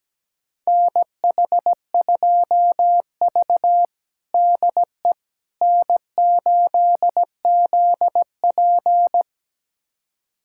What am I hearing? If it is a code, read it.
Morse code, heard as a keyed tone, about 17 wpm, 700 Hz. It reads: NH2V DE N8ZP
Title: USA Callsign DE Callsign - 4 Characters - 17wpm